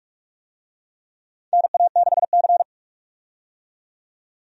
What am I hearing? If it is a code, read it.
Morse code, heard as a keyed tone, about 41 wpm, 700 Hz. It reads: NA6C
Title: USA Call Signs - 4 Characters - 45wpm